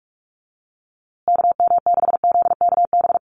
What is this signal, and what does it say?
Morse code, heard as a keyed tone, about 46 wpm, 700 Hz. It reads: KM6ZKB
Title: USA Call Signs Repeated 2x 45wpm